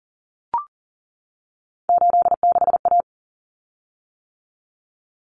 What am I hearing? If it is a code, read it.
Morse code, heard as a keyed tone, about 40 wpm, 700 Hz. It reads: 86A
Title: Common Cut Numbers - 3-Digit 40wpm